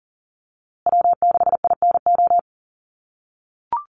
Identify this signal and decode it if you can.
Morse code, heard as a keyed tone, about 40 wpm, 700 Hz. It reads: W6INO
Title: USA Call Signs Repeated 2x 40wpm